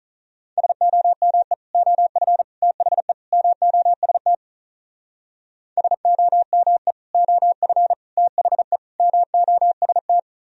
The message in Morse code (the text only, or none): SOME OF THE MOST SOME OF THE MOST